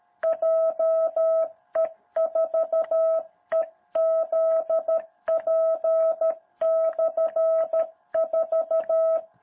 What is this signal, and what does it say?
Morse code, heard as a keyed tone, about 13 wpm, 660 Hz. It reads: JE4EZP/4
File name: jh8cbh / 2021cw